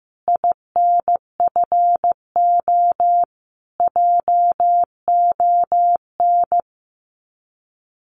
Morse code, heard as a keyed tone, about 15 wpm, 700 Hz. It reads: INFO JON
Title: Common QSO Phrases Encore 15wpm